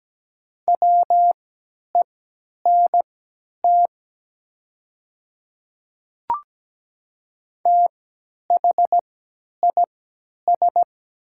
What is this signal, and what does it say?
Morse code, heard as a keyed tone, about 17 wpm, spaced as for 10 wpm, 700 Hz. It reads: WENT THIS
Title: 3x Character Spacing - Top 200 Words - 17wpm